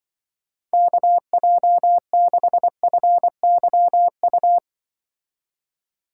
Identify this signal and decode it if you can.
Morse code, heard as a keyed tone, about 23 wpm, 700 Hz. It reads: KJ6FYU
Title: Speed Racing - USA 2x3 Call Signs 20wpm